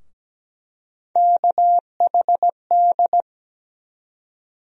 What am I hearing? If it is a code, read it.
Morse code, heard as a keyed tone, about 17 wpm, 700 Hz. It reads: KHD
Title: USA Call Sign Postfixes - 3 Letters 17wpm